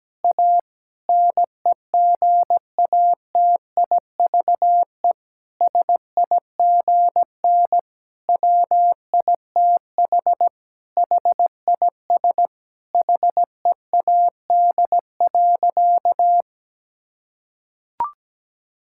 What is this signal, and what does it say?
Morse code, heard as a keyed tone, about 17 wpm, 700 Hz. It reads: A NEGATIVE SIGN WITH HIS HEAD.
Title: Sentences from Top 1000 Words - Review Part 6 - 17wpm